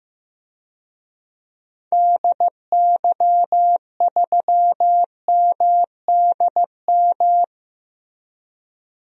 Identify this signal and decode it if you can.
Morse code, heard as a keyed tone, about 15 wpm, 700 Hz. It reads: DY3MDM
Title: DXCC-100 Call Signs - 6 Characters - Repeated 2x - 15wpm